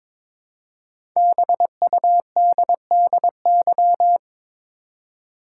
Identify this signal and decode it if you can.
Morse code, heard as a keyed tone, about 23 wpm, 700 Hz. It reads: BUDDY
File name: 5 Letter Words - First 3 Letters Given - 22wpm